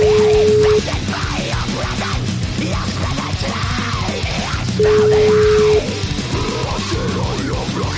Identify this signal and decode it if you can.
Ringback tone (Europe, 425 Hz), with noise.